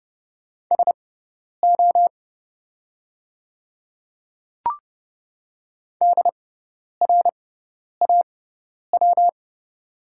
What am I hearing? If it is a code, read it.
Morse code, heard as a keyed tone, about 30 wpm, spaced as for 10 wpm, 700 Hz. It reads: SO DRAW
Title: 6x Character Spacing - Top 300 Words - 30wpm